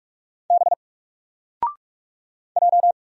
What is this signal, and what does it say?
Morse code, heard as a keyed tone, about 43 wpm, 700 Hz. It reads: B J
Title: Single Character (Letter:Number:Punctuation:Prosign) Alt 45wpm